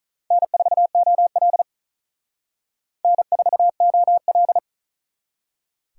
Morse code, heard as a keyed tone, about 43 wpm, 700 Hz. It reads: N4OL N4OL
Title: Speed Racing - USA 1x2 Call Signs 35wpm